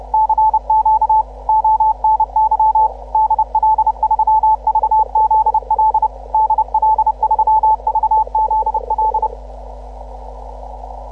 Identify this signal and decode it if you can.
Morse code, heard as a keyed tone, about 30 wpm, 870 Hz. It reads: CQ ONY DL3VZL DL3VZL